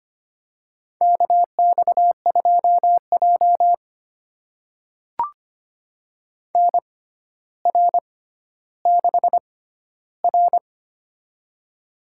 Morse code, heard as a keyed tone, about 25 wpm, spaced as for 13 wpm, 700 Hz. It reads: KX2J NR6R
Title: USA Call Signs - 4 Characters - 6x Character Spacing - 25wpm